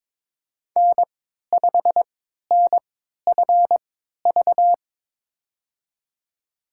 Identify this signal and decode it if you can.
Morse code, heard as a keyed tone, about 22 wpm, spaced as for 13 wpm, 700 Hz. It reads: N5NFV
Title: USA Call Signs - 5 Characters - 3x Character Spacing - 22wpm